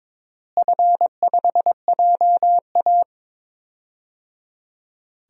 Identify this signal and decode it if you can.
Morse code, heard as a keyed tone, about 22 wpm, 700 Hz. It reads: F5JA
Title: DXCC-100 Call Signs - 4 Characters - Repeated 3x - 22wpm